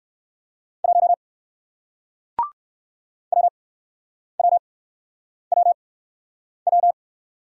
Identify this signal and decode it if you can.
Morse code, heard as a keyed tone, about 60 wpm, spaced as for 9 wpm, 700 Hz. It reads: 1 WWWW